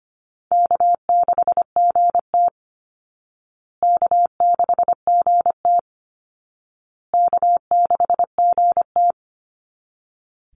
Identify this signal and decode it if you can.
Morse code, heard as a keyed tone, about 25 wpm, 700 Hz. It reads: K6GT K6GT K6GT
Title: Easy Call Signs Repeated 3x - 25wpm